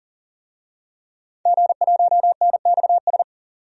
Morse code, heard as a keyed tone, about 38 wpm, 700 Hz. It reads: G1NXS
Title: DXCC-100 Call Signs - 5 Characters - 40wpm